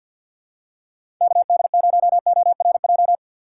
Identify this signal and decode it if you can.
Morse code, heard as a keyed tone, about 52 wpm, 700 Hz. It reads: KD0ORJ
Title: USA Call Signs - 6 Characters - 6x Character Spacing - 50wpm